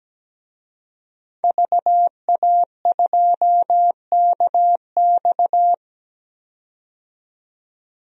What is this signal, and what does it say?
Morse code, heard as a keyed tone, about 17 wpm, 700 Hz. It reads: VA2KX
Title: DXCC-100 Call Signs - 5 Characters - 17wpm